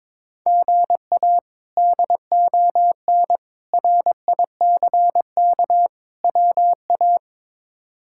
Morse code, heard as a keyed tone, about 22 wpm, 700 Hz. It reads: GA DON RICK WA
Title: K1USN SST QSO Practice - 22wpm